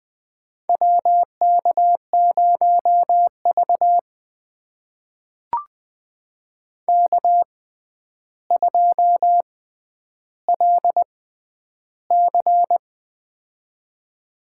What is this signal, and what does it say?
Morse code, heard as a keyed tone, about 20 wpm, spaced as for 10 wpm, 700 Hz. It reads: WK0V K2LC
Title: USA Call Signs - 4 Characters - 6x Character Spacing - 20wpm